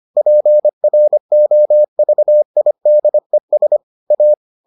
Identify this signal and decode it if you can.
Morse code, heard as a keyed tone, about 25 wpm, 600 Hz. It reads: PROVIDES A